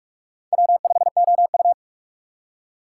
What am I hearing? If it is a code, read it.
Morse code, heard as a keyed tone, about 41 wpm, 700 Hz. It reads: W5OU
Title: USA Call Signs - 3x Character Spacing - 45wpm